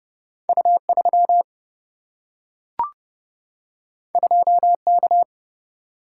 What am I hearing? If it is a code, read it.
Morse code, heard as a keyed tone, about 30 wpm, 700 Hz. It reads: U 3 2 K
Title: Single-Letter Single-Number 30wpm